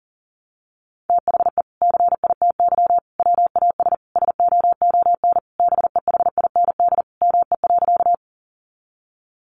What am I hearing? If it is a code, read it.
Morse code, heard as a keyed tone, about 40 wpm, 700 Hz. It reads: THE CITY WAS SOON BEHIND ME.
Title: Sentences from Top 500 Words 40wpm